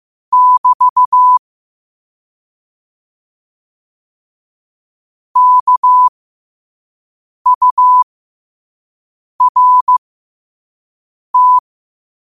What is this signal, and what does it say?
Morse code, heard as a keyed tone, about 15 wpm, spaced as for 5 wpm, 1000 Hz. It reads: <BT> KURT
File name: qotdfiles / 5wpm